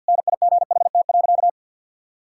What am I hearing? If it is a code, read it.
Morse code, heard as a keyed tone, about 48 wpm, 700 Hz. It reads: NIGHT.
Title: Sentences from Top 1000 Words - Review Part 6 - 50wpm